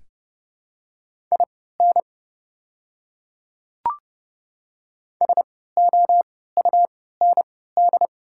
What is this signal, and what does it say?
Morse code, heard as a keyed tone, about 30 wpm, spaced as for 17 wpm, 700 Hz. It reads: IN SOUND
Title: 3x Character Spacing - Top 200 Words - 30wpm